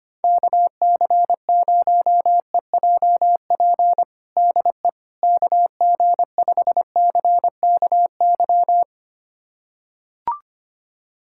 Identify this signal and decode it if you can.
Morse code, heard as a keyed tone, about 25 wpm, 700 Hz. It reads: KC0EJP DE KG5CKY
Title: USA Callsign DE Callsign - 6 Characters - 25wpm